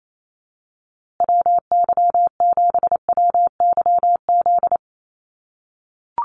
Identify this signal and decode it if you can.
Morse code, heard as a keyed tone, about 28 wpm, 700 Hz. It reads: WY7WYZ
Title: USA Call Signs Repeated 2x 28wpm